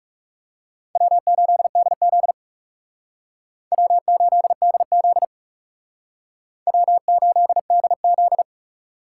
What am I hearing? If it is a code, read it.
Morse code, heard as a keyed tone, about 39 wpm, 700 Hz. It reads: W8DZ W8DZ W8DZ